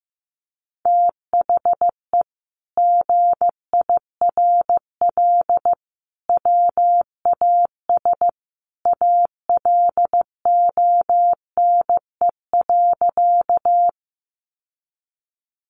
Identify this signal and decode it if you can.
Morse code, heard as a keyed tone, about 15 wpm, 700 Hz. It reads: THE GIRL WAS ALONE.
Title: Sentences from Top 1000 Words 15wpm